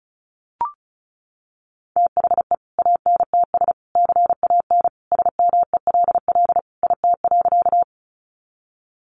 Morse code, heard as a keyed tone, about 34 wpm, 700 Hz. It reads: THE ANTS CAN SMELL IT.
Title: Easy Phrases 35wpm